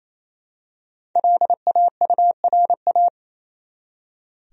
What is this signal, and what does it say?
Morse code, heard as a keyed tone, about 28 wpm, 700 Hz. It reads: LAURA
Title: USA Common Names - Max 5 Letters - 28wpm